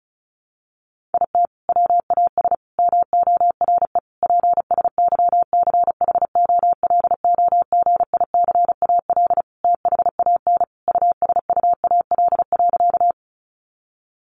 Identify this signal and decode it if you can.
Morse code, heard as a keyed tone, about 34 wpm, 700 Hz. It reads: IT WAS MORE PSYCHOLOGICAL THAN USUAL.
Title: Sentences from Top 1400 Words 35wpm